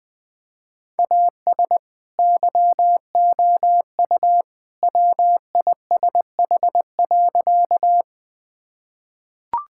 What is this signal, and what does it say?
Morse code, heard as a keyed tone, about 21 wpm, 700 Hz. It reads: AS YOU WISH.
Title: Sentences from Top 1600 Words 20wpm